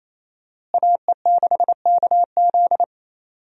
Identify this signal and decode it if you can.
Morse code, heard as a keyed tone, about 28 wpm, 700 Hz. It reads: AE6KZ
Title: USA Call Signs - 5 Characters - Repeated 2x - 28wpm